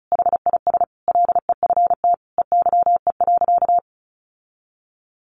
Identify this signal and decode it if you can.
Morse code, heard as a keyed tone, about 34 wpm, 700 Hz. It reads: HIS LEFT EYE.
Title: Sentences from Top 1400 Words 35wpm